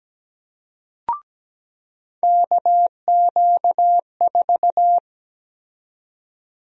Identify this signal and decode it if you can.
Morse code, heard as a keyed tone, about 17 wpm, 700 Hz. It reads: KQ4
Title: USA Call Sign Prefixes - 17wpm